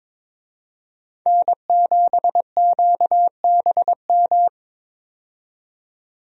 Morse code, heard as a keyed tone, about 22 wpm, 700 Hz. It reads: N7QBM
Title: USA Call Signs - 5 Characters - 22wpm